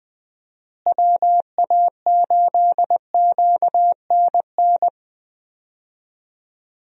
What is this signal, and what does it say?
Morse code, heard as a keyed tone, about 20 wpm, 700 Hz. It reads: WA8QNN
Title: USA Call Signs - 6 Characters - Repeated 3x - 20wpm